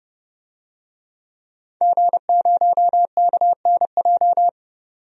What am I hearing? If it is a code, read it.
Morse code, heard as a keyed tone, about 30 wpm, 700 Hz. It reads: G0KNJ
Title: DXCC-100 Call Signs - 5 Characters - 30wpm